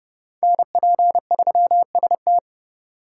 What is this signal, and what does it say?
Morse code, heard as a keyed tone, about 29 wpm, 700 Hz. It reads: NP3ST